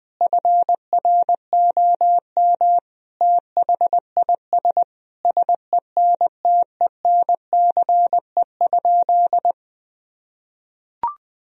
Morse code, heard as a keyed tone, about 20 wpm, 700 Hz. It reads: FROM THIS SENTENCE?
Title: Sentences from Top 1000 Words - Review Part 1 - 20wpm